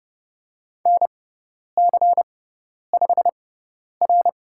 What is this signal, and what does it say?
Morse code, heard as a keyed tone, about 30 wpm, spaced as for 10 wpm, 700 Hz. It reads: NC5R
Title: USA Call Signs - 4 Characters - 6x Character Spacing - 30wpm